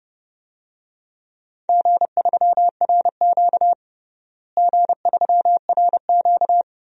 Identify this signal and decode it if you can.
Morse code, heard as a keyed tone, about 32 wpm, 700 Hz. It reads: G3RQ G3RQ
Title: DXCC-100 Call Signs - 4 Characters - Repeated 3x - 30wpm